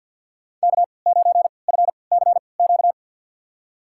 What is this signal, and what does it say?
Morse code, heard as a keyed tone, about 53 wpm, spaced as for 29 wpm, 700 Hz. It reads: K9FCQ
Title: USA Call Signs - 3x Character Spacing - 50wpm